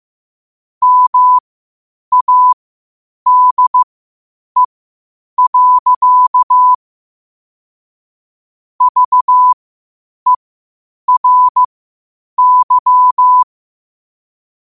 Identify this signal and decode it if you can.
Morse code, heard as a keyed tone, about 15 wpm, spaced as for 8 wpm, 1000 Hz. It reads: MADE. VERY
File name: qotdfiles / 8wpm